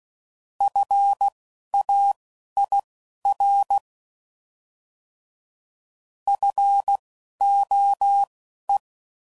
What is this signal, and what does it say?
Morse code, heard as a keyed tone, about 16 wpm, spaced as for 11 wpm, 800 Hz. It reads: FAIR FOE
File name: Prefixes / By First Letter / Fs